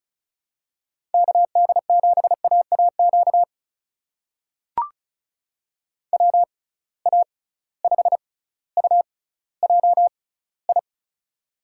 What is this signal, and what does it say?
Morse code, heard as a keyed tone, about 36 wpm, spaced as for 18 wpm, 700 Hz. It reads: KD7AAQ WA5UJI
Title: USA Call Signs - 6 Characters - 6x Character Spacing - 35wpm